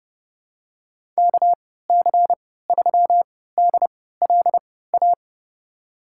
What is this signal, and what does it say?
Morse code, heard as a keyed tone, about 31 wpm, spaced as for 17 wpm, 700 Hz. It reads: KC3DLA
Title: USA Call Signs - 6 Characters - 3x Character Spacing - 30wpm